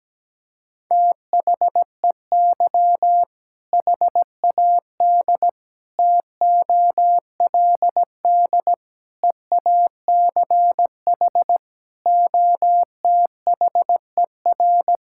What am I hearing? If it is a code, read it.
Morse code, heard as a keyed tone, about 17 wpm, 700 Hz. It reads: THEY HAD TOLD EACH OTHER
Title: Sentences from Top 1600 Words 17wpm